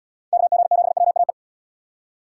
Morse code, heard as a keyed tone, about 69 wpm, 700 Hz. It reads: KG7WAE